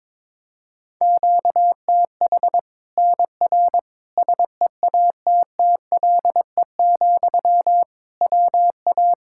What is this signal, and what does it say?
Morse code, heard as a keyed tone, about 22 wpm, 700 Hz. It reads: QTH NR SEATTLE, WA
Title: Practice QSO - Exchange 1 - Beginning - 22wpm